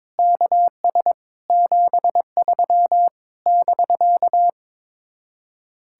Morse code, heard as a keyed tone, about 22 wpm, 700 Hz. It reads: KS 73 <BK>
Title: Speed-Racing Common QSO Phrases - 22wpm